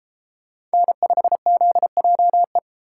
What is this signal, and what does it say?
Morse code, heard as a keyed tone, about 31 wpm, 700 Hz. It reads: N5ZJE